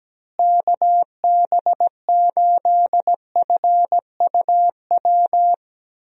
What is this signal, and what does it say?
Morse code, heard as a keyed tone, about 17 wpm, 700 Hz. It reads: KB8FUW
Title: USA Call Signs - 6 Characters - Repeated 3x - 17wpm